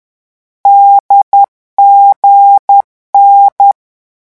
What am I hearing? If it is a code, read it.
Morse code, heard as a keyed tone, about 10 wpm, 800 Hz. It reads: DGN
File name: morse mp3 / lecon4-vitesse 10